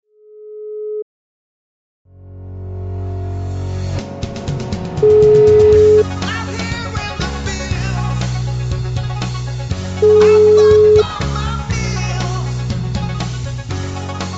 Ringback tone (Europe, 425 Hz).